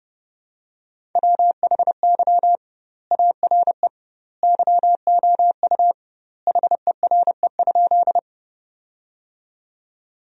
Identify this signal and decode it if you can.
Morse code, heard as a keyed tone, about 30 wpm, 700 Hz. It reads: WHY ARE YOU HERE?
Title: 2x Word-Spacing - Sentences from Top 200 Words - 30wpm